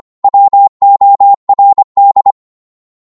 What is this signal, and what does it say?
Morse code, heard as a keyed tone, about 25 wpm, 800 Hz. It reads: WORD